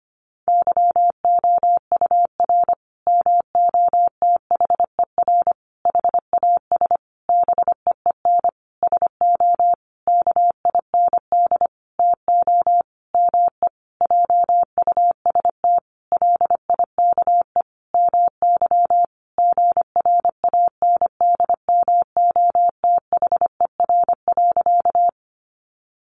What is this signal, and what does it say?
Morse code, heard as a keyed tone, about 25 wpm, 700 Hz. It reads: YOUR MOTHER HAS BEEN SO KIND TO ME JUST LIKE MY GRANDMOTHER.
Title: Sentences from Top 1200 Words 25wpm